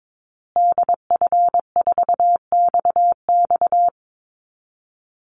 Morse code, heard as a keyed tone, about 22 wpm, 700 Hz. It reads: DF4XX
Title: Call Signs Repeated 3x - 22wpm